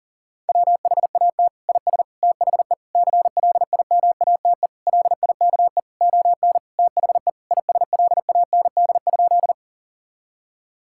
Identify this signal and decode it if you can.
Morse code, heard as a keyed tone, about 40 wpm, 700 Hz. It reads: WHAT IS THE CLIMATE LIKE ON THE ISLAND?
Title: Sentences from Top 1600 Words 40wpm